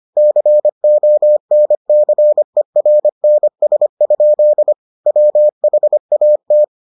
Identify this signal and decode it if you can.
Morse code, heard as a keyed tone, about 25 wpm, 600 Hz. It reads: CONCERNS? WHAT